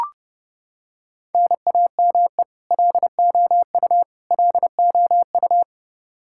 Morse code, heard as a keyed tone, about 30 wpm, 700 Hz. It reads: NAME LOU LOU
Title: Practice QSO - Exchange 1 - Beginning - 30wpm